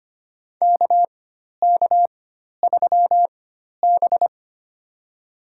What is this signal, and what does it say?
Morse code, heard as a keyed tone, about 25 wpm, spaced as for 12 wpm, 700 Hz. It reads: KK3B
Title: USA Call Signs - 4 Characters - 4x Character Spacing - 25wpm